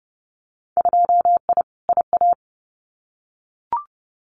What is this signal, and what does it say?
Morse code, heard as a keyed tone, about 30 wpm, 700 Hz. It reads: 2I IA
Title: Winter Field Day Exchanges Encore 30wpm